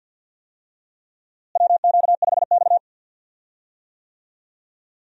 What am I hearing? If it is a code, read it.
Morse code, heard as a keyed tone, about 46 wpm, 700 Hz. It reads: WQ5X
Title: USA Call Signs - 4 Characters - Repeated 2x - 50wpm